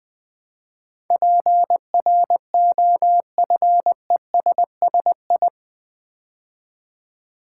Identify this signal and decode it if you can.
Morse code, heard as a keyed tone, about 20 wpm, 700 Hz. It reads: PROFESSI
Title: Build A Word - 4 Letters At A Time - 12-Letter Words - 20wpm